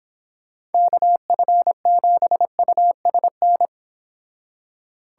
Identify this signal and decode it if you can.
Morse code, heard as a keyed tone, about 26 wpm, 700 Hz. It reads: KF7USN